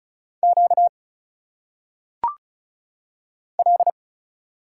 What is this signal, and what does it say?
Morse code, heard as a keyed tone, about 34 wpm, 700 Hz. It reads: Q L
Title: Single Letters - Phonetic Pronunciation 35wpm